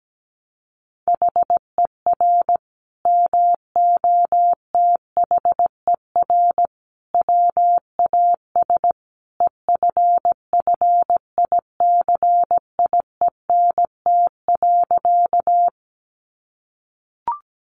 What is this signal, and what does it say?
Morse code, heard as a keyed tone, about 17 wpm, 700 Hz. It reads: HER MOTHER WAS EFFICIENT.
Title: Sentences from Top 1300 Words 17wpm